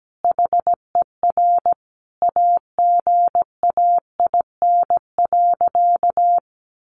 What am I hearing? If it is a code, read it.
Morse code, heard as a keyed tone, about 17 wpm, 700 Hz. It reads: HER AGAIN.
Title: Sentences from Top 1500 Words 17wpm